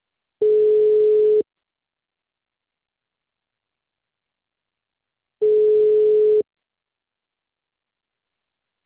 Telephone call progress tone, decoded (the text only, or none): ringback tone